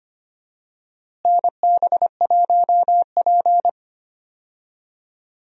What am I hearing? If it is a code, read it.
Morse code, heard as a keyed tone, about 25 wpm, 700 Hz. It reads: NB1P
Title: USA Call Signs - 4 Characters - Repeated 3x - 25wpm